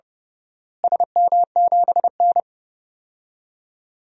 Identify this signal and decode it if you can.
Morse code, heard as a keyed tone, about 30 wpm, 700 Hz. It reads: SM7N
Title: DXCC-100 Call Signs - 4 Characters - Repeated 2x - 30wpm